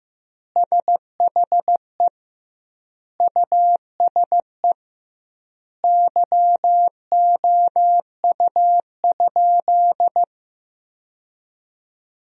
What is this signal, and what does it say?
Morse code, heard as a keyed tone, about 15 wpm, 700 Hz. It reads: SHE USE YOU?
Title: 2x Word-Spacing - Sentences from Top 200 Words - 15wpm